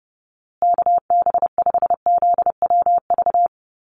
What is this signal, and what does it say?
Morse code, heard as a keyed tone, about 30 wpm, 700 Hz. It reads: KB5ZWV
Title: USA Call Signs Repeated 3x 30wpm